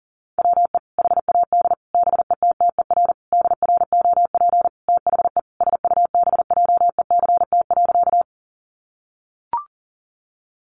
Morse code, heard as a keyed tone, about 40 wpm, 700 Hz. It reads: WE HAD BETTER DROP THE SUBJECT.
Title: Sentences from Top 1400 Words 40wpm